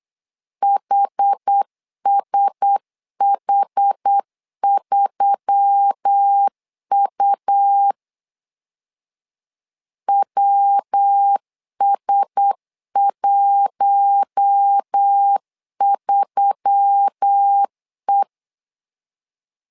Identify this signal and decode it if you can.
Morse code, heard as a keyed tone, about 8 wpm, 800 Hz. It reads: HSH3U WS13E
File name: LEZIONI / 4 Q1S3 8wpm 3037